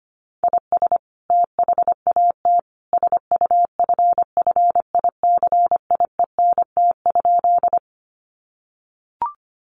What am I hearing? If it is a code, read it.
Morse code, heard as a keyed tone, about 25 wpm, 700 Hz. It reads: IS THAT SUFFICIENT?
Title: Sentences from Top 1400 Words 25wpm